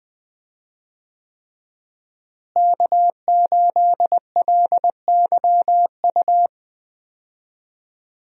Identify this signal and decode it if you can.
Morse code, heard as a keyed tone, about 20 wpm, 700 Hz. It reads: K8LYU
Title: USA Call Signs - 5 Characters - Repeated 3x - 20wpm